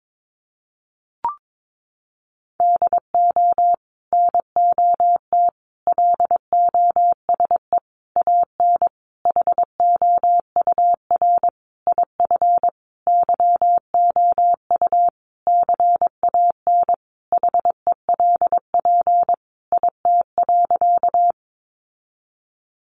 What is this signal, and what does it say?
Morse code, heard as a keyed tone, about 22 wpm, 700 Hz. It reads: DO NOT LOSE AN HOUR IF YOU CAN HELP IT.
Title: Sentences from Top 1100 Words 22wpm